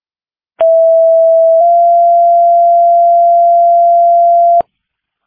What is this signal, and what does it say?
Two-tone page, heard as a one-second tone, then a three-second tone, about 660 then 690 Hz. It is heard from 0.6 to 4.6 s.